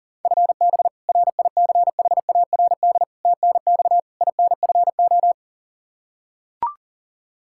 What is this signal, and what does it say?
Morse code, heard as a keyed tone, about 40 wpm, 700 Hz. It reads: FB RICHARD TNX INFO
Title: Practice QSO - Exchange 3 - More Info - 40wpm